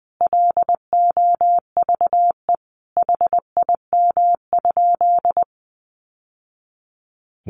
Morse code, heard as a keyed tone, about 20 wpm, 700 Hz. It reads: LOVE HIM?
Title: Sentences from Top 1100 Words 20wpm